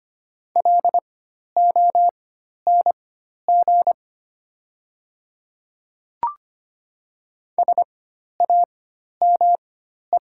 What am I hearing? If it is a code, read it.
Morse code, heard as a keyed tone, about 25 wpm, spaced as for 12 wpm, 700 Hz. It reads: LONG SAME
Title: 4x Character Spacing - Top 200 Words - 25wpm